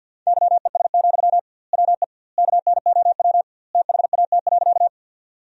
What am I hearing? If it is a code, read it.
Morse code, heard as a keyed tone, about 50 wpm, 700 Hz. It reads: YES, WE KNOW THAT.
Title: 2x Word-Spacing - Sentences from Top 500 Words - 50wpm